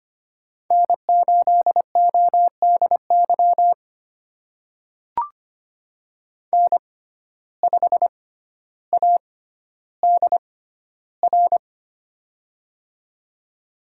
Morse code, heard as a keyed tone, about 25 wpm, spaced as for 13 wpm, 700 Hz. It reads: N8ODY N5ADR
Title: USA Call Signs - 5 Characters - 6x Character Spacing - 25wpm